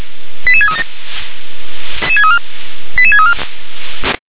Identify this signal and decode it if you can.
Five-tone selective call (ZVEI), heard as quick runs of five tones, tones of about 70 ms. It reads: 88637 88634
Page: SDR/ZVEI